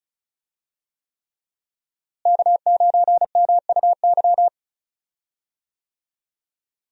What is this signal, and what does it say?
Morse code, heard as a keyed tone, about 36 wpm, 700 Hz. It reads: K9MUY
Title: USA Call Signs - 5 Characters - Repeated 3x - 35wpm